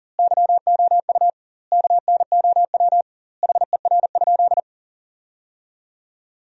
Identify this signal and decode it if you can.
Morse code, heard as a keyed tone, about 40 wpm, 700 Hz. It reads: YOU KNOW HER?
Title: 2x Word-Spacing - Sentences from Top 100 Words - 40wpm